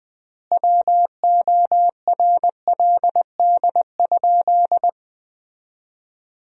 Morse code, heard as a keyed tone, about 20 wpm, 700 Hz. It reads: WORLD?
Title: 3x Word-Spacing - Sentences from Top 200 Words - 20wpm